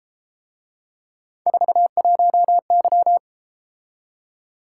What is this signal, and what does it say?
Morse code, heard as a keyed tone, about 33 wpm, 700 Hz. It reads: 41Y